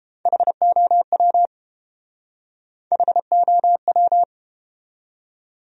Morse code, heard as a keyed tone, about 30 wpm, 700 Hz. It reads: HOW HOW